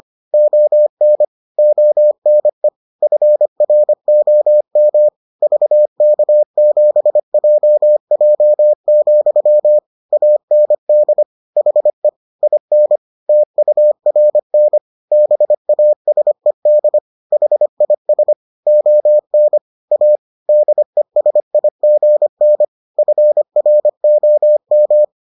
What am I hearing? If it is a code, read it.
Morse code, heard as a keyed tone, about 25 wpm, 600 Hz. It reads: ON ONE FROM VK7JJ, AND HE IN TURN BASED HIS ON A DESIGN FROM